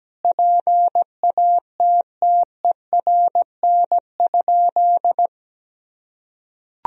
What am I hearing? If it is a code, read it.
Morse code, heard as a keyed tone, about 17 wpm, 700 Hz. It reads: PATTERN?
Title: Sentences from Top 1000 Words - Review Part 2 - 17wpm